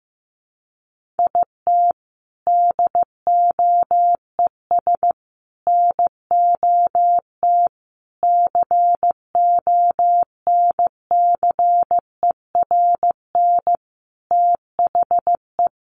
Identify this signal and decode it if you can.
Morse code, heard as a keyed tone, about 15 wpm, 700 Hz. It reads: IT DOES NOT CONCERN THE
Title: Sentences from Top 1500 Words 15wpm